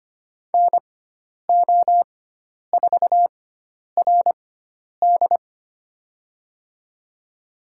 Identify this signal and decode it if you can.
Morse code, heard as a keyed tone, about 25 wpm, spaced as for 10 wpm, 700 Hz. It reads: NO4RD
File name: Lesson 16g - Letter c - Call Signs 25wpm